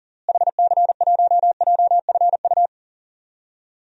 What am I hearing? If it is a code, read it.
Morse code, heard as a keyed tone, about 38 wpm, 700 Hz. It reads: HC1JFU
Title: DXCC-100 Call Signs - 6 Characters - Repeated 3x - 40wpm